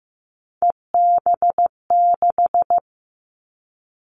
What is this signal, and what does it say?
Morse code, heard as a keyed tone, about 15 wpm, 700 Hz. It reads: EB6
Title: Callsign Prefixes for Top 100 DXCC - 15wpm